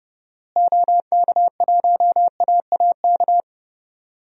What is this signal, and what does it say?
Morse code, heard as a keyed tone, about 30 wpm, 700 Hz. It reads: OK1AAK
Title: DXCC-100 Call Signs - 6 Characters - Repeated 2x - 30wpm